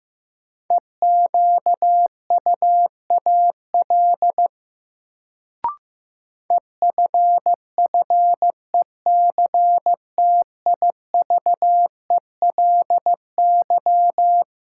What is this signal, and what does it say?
Morse code, heard as a keyed tone, about 15 wpm, 700 Hz. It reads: EQUAL EFFECTIVELY
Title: Sentences from Top 1400 Words - New Word Drill - 15wpm